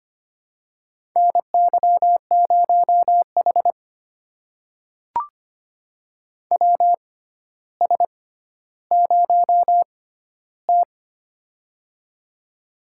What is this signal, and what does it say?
Morse code, heard as a keyed tone, about 25 wpm, spaced as for 13 wpm, 700 Hz. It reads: NY0H WS0T
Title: USA Call Signs - 4 Characters - 6x Character Spacing - 25wpm